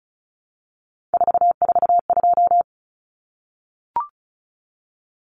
Morse code, heard as a keyed tone, about 34 wpm, 700 Hz. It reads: 442
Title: Numbers - 3-Digit 35wpm